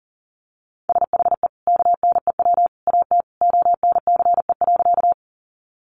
Morse code, heard as a keyed tone, about 40 wpm, 700 Hz. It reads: SHE KNEW AT ONCE.
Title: Sentences from Top 400 Words 40wpm